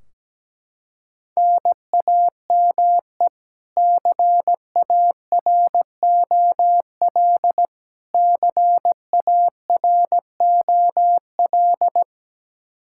Morse code, heard as a keyed tone, about 17 wpm, 700 Hz. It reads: NAME CAROL CAROL
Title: Practice QSO - Exchange 1 - Beginning - 17wpm